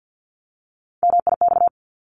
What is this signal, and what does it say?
Morse code, heard as a keyed tone, about 50 wpm, 700 Hz. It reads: MIX
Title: Top 751-1000 Words 50wpm